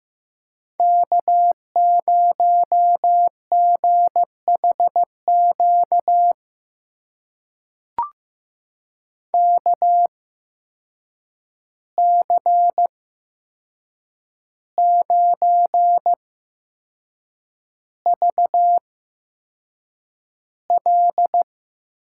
Morse code, heard as a keyed tone, about 15 wpm, spaced as for 6 wpm, 700 Hz. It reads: K0GHQ KC9VL
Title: USA Call Signs - 5 Characters - 8x Character Spacing - 15wpm